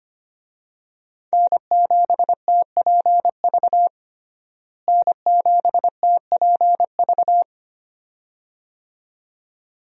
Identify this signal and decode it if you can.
Morse code, heard as a keyed tone, about 25 wpm, 700 Hz. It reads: N7TPV N7TPV
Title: USA Call Signs - 5 Characters - Repeated 2x - 25wpm